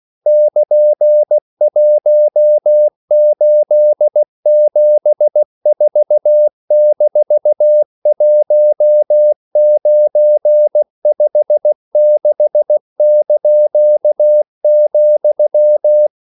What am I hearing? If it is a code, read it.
Morse code, heard as a keyed tone, about 16 wpm, 600 Hz. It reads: <KN>1874-1956),
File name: CW / Practice / EnglishQuotations0416